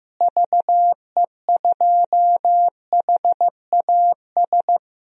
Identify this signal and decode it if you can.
Morse code, heard as a keyed tone, about 15 wpm, 700 Hz. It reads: VE2HAS
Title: DXCC-100 Call Signs - 6 Characters - Repeated 3x - 15wpm